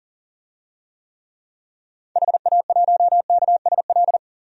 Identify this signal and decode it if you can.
Morse code, heard as a keyed tone, about 37 wpm, 700 Hz. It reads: HA1KSL
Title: DXCC-100 Call Signs - 6 Characters - 40wpm